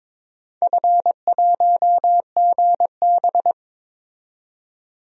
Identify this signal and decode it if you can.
Morse code, heard as a keyed tone, about 21 wpm, 700 Hz. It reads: F1GB
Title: DXCC-100 Call Signs - 4 Characters - 22wpm